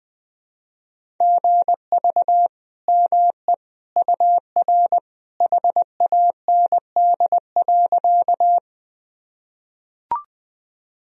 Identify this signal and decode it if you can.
Morse code, heard as a keyed tone, about 21 wpm, 700 Hz. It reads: GV ME UR HAND.
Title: Sentences from Top 200 Words and Abbreviations 20wpm